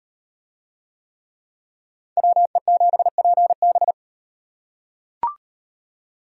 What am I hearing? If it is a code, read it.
Morse code, heard as a keyed tone, about 34 wpm, 700 Hz. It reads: WE7PB